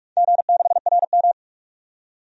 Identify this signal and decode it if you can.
Morse code, heard as a keyed tone, about 44 wpm, 690 Hz. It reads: G6RM
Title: DXCC-100 Call Signs - 4 Characters - 45wpm